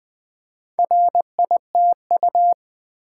Morse code, heard as a keyed tone, about 20 wpm, 700 Hz. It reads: RITU
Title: Build A Word - 2 Letters At A Time - 6-Letter Words - 20wpm